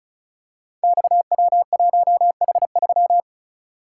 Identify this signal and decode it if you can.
Morse code, heard as a keyed tone, about 36 wpm, 700 Hz. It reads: XW1H3
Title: Code Groups - 5 Letter:Numbers - 35wpm